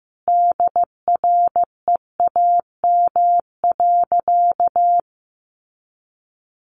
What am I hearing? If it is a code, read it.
Morse code, heard as a keyed tone, about 15 wpm, 700 Hz. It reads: DREAM.
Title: Sentences from Top 1500 Words 15wpm